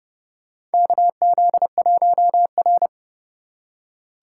Morse code, heard as a keyed tone, about 30 wpm, 700 Hz. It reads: KZ1R
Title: USA Call Signs - 4 Characters - 4x Character Spacing - 30wpm